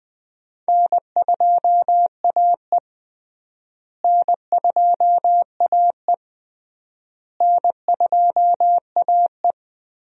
Morse code, heard as a keyed tone, about 20 wpm, 700 Hz. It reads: N2AE N2AE N2AE
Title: USA Call Signs - 4 Characters - Repeated 3x - 20wpm